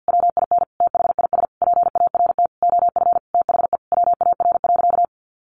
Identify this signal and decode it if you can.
Morse code, heard as a keyed tone, about 50 wpm, 700 Hz. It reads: WIN THIS PART OF THE WAR.
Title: Sentences from Top 1500 Words 50wpm